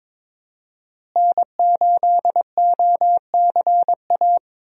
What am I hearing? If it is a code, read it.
Morse code, heard as a keyed tone, about 22 wpm, 700 Hz. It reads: N8OCA
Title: USA Call Signs - 5 Characters - Repeated 3x - 22wpm